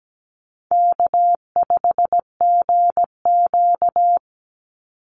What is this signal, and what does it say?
Morse code, heard as a keyed tone, about 17 wpm, 700 Hz. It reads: K5GQ
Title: Call Signs - 17wpm